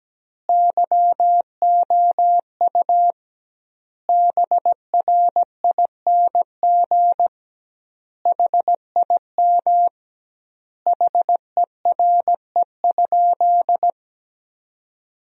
Morse code, heard as a keyed tone, about 17 wpm, 700 Hz. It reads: YOU BRING HIM HERE?
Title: 2x Word-Spacing - Sentences from Top 500 Words - 17wpm